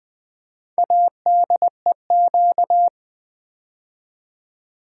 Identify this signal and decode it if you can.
Morse code, heard as a keyed tone, about 20 wpm, 700 Hz. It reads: ADEQ
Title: Build A Word - 4 Letters At A Time - 8-Letter Words - 20wpm